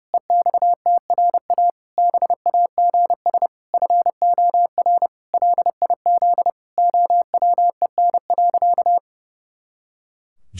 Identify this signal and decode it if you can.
Morse code, heard as a keyed tone, about 32 wpm, 700 Hz. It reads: EXTRA BAGS FOR LIZ OWEN.
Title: Pangram Challenge v2 - 30wpm